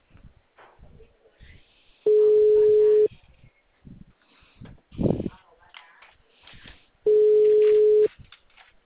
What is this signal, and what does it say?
Ringback tone (Europe, 425 Hz), with speech.